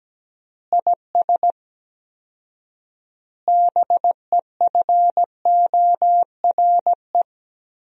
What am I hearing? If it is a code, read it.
Morse code, heard as a keyed tone, about 17 wpm, 700 Hz. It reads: IS BEFORE
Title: 4x Word-Spacing - Sentences from Top 200 Words - 17wpm